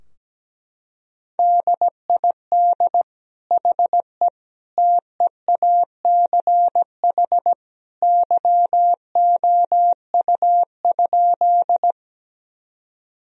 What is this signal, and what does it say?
Morse code, heard as a keyed tone, about 17 wpm, 700 Hz. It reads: DID HE TEACH YOU?
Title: Sentences from Top 1000 Words - Review Part 2 - 17wpm